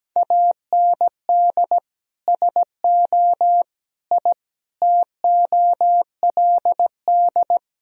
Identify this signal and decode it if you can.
Morse code, heard as a keyed tone, about 17 wpm, 700 Hz. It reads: AND SO I TOLD
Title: Sentences from Top 1000 Words - Review Part 4 - 17wpm